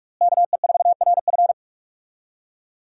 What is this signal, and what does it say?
Morse code, heard as a keyed tone, about 46 wpm, 700 Hz. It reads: KE4RF
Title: USA Call Signs - 5 Characters - Repeated 3x - 45wpm